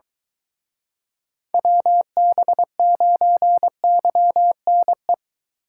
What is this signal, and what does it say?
Morse code, heard as a keyed tone, about 22 wpm, 700 Hz. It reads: WB9YNE